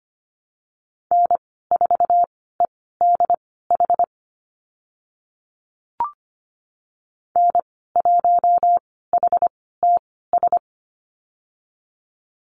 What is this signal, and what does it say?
Morse code, heard as a keyed tone, about 25 wpm, spaced as for 16 wpm, 700 Hz. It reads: N4EDH N1HTS
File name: Lesson 15g - Number 5 - Call Signs 25wpm